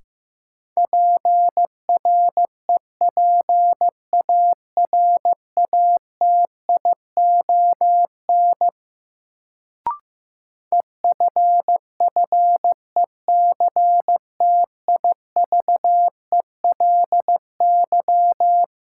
Morse code, heard as a keyed tone, about 15 wpm, 700 Hz. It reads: PREPARATION EFFECTIVELY
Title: Sentences from Top 1400 Words - New Word Drill - 15wpm